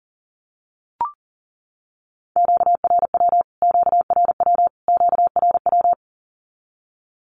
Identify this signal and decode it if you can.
Morse code, heard as a keyed tone, about 40 wpm, 700 Hz. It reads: QRW QRW QRW
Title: Q-Codes Repeated 3x 40wpm